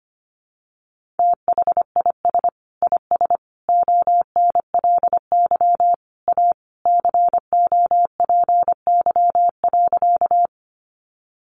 Morse code, heard as a keyed tone, about 25 wpm, 700 Hz. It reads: THIS IS ONLY A COPY.
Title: Sentences from Top 1400 Words 25wpm